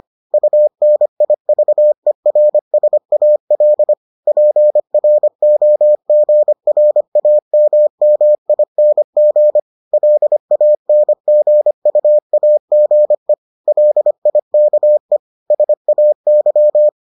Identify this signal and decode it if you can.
Morse code, heard as a keyed tone, about 25 wpm, 600 Hz. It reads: UNIVERSAL PROGRAMMING LANGUAGE LIKE SAY